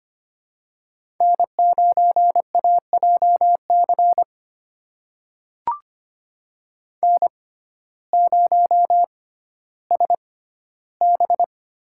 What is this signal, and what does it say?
Morse code, heard as a keyed tone, about 25 wpm, spaced as for 14 wpm, 700 Hz. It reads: N9AJC N0SB
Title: USA Call Signs - 5 Characters - 6x Character Spacing - 25wpm